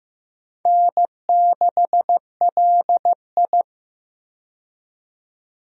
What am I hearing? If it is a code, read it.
Morse code, heard as a keyed tone, about 15 wpm, 700 Hz. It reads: N6LI
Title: USA Call Signs - 4 Characters - 15wpm